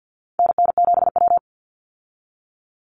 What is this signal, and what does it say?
Morse code, heard as a keyed tone, about 50 wpm, 700 Hz. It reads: NN7W
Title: Call Signs - 50wpm